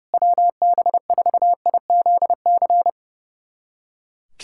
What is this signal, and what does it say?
Morse code, heard as a keyed tone, about 30 wpm, 700 Hz. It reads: WB4IZC
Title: USA Callsign DE Callsign - 6 Characters - 30wpm